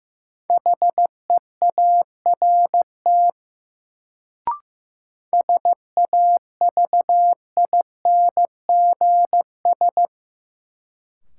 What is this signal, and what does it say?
Morse code, heard as a keyed tone, about 15 wpm, 700 Hz. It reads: HEART SAVINGS
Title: Sentences from Top 1200 Words - New Word Drill - 15wpm